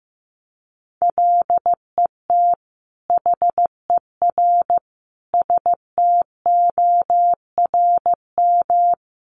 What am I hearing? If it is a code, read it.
Morse code, heard as a keyed tone, about 15 wpm, 700 Hz. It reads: LET HER STORM
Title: Sentences from Top 1500 Words 15wpm